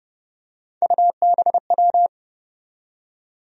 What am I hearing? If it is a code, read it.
Morse code, heard as a keyed tone, about 29 wpm, 700 Hz. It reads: UBW
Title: USA Call Sign Postfixes - 3 Letters 30wpm